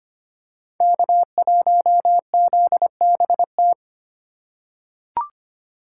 Morse code, heard as a keyed tone, about 25 wpm, 700 Hz. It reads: K1ZBT
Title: USA Call Signs - 5 Characters - 4x Character Spacing - 25wpm